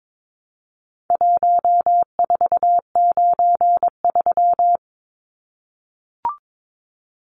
Morse code, heard as a keyed tone, about 22 wpm, 700 Hz. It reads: 1493
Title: Numbers - Contest Style - 4-Digit 22wpm